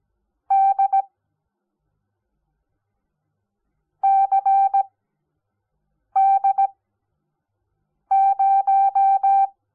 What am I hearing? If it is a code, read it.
Morse code, heard as a keyed tone, about 17 wpm, 780 Hz. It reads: D C D 0
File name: Onlinelehrgang / morsen / DJ4UF Morsen05